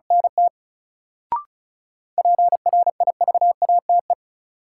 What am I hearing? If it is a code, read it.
Morse code, heard as a keyed tone, about 36 wpm, 700 Hz. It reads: NT PRIVATE
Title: Sentences from Top 1000 Words - New Word Drill - 35wpm